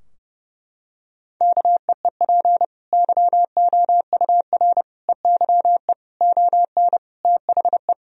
Morse code, heard as a keyed tone, about 32 wpm, 700 Hz. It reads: KEEP YOUR EYE ON THE
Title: Sentences from Top 1000 Words - Review Part 4 - 30wpm